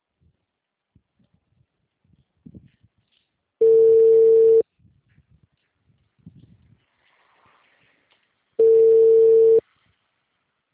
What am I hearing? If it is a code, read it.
Ringback tone (China, 450 Hz).
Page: Rio Claro 2 - SP